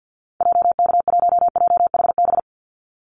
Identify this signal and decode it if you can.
Morse code, heard as a keyed tone, about 50 wpm, 700 Hz. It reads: JK1JHB
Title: Call Signs Repeated 2x - 50wpm